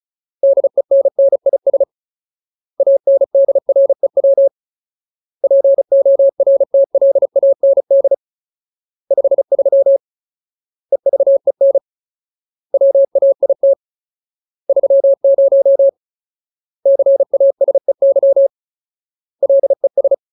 Morse code, heard as a keyed tone, about 34 wpm, 550 Hz. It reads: DENNIS ANDREW PORTLAND 53 EVEN WAIT 30 CASEY LES
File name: Ultimate Word List / 35WPM Ultimate